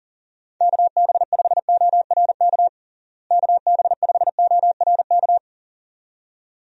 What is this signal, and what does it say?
Morse code, heard as a keyed tone, about 40 wpm, 700 Hz. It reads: KB5ORK KB5ORK
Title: USA Call Signs - 6 Characters - Repeated 2x - 40wpm